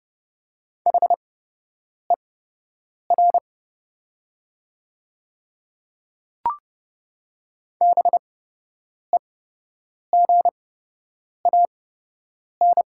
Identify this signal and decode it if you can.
Morse code, heard as a keyed tone, about 29 wpm, spaced as for 8 wpm, 690 Hz. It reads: HER BEGAN
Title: 8x Character Spacing - Top 500 Words - 30wpm